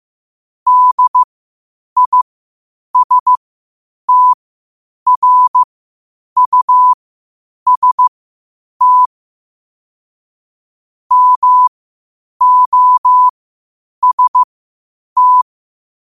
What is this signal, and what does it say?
Morse code, heard as a keyed tone, about 15 wpm, spaced as for 8 wpm, 1000 Hz. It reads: DISTRUST MOST
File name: qotdfiles / 8wpm